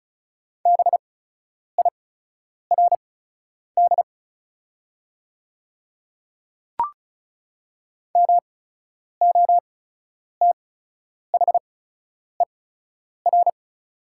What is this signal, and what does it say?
Morse code, heard as a keyed tone, about 36 wpm, spaced as for 10 wpm, 700 Hz. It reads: BIRD MOTHER
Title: 8x Character Spacing - Top 500 Words - 35wpm